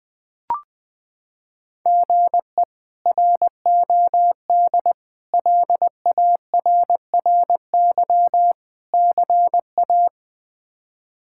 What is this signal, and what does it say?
Morse code, heard as a keyed tone, about 20 wpm, 700 Hz. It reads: GE ROD LARRY CA
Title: K1USN SST QSO Practice - 20wpm